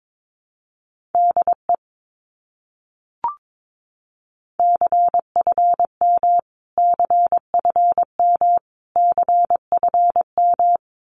Morse code, heard as a keyed tone, about 22 wpm, 700 Hz. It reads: DE CFM CFM CFM
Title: Abbreviations Repeated 3x - 22wpm